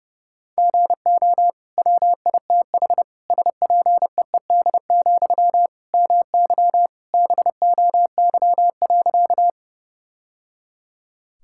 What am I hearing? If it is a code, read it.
Morse code, heard as a keyed tone, about 30 wpm, 700 Hz. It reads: GO WITH SPEED, MY BOY.
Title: Sentences from Top 1000 Words - Review Part 5 - 30wpm